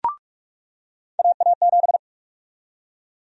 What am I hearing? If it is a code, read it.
Morse code, heard as a keyed tone, about 46 wpm, 700 Hz. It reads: AA7
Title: USA Call Sign Prefixes - 45wpm